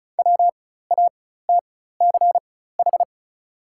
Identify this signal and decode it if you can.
Morse code, heard as a keyed tone, about 34 wpm, spaced as for 16 wpm, 700 Hz. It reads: WATCH
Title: 4x Character Spacing - Top 500 Words - 35wpm